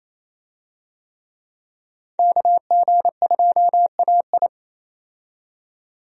Morse code, heard as a keyed tone, about 27 wpm, 700 Hz. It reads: KG2AI
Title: USA Call Signs - 5 Characters - 28wpm